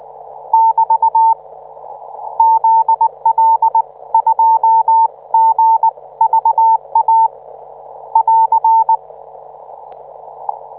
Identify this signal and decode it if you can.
Morse code, heard as a keyed tone, about 19 wpm, spaced as for 17 wpm, 900 Hz. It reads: <BT> ZL2GVA <AR>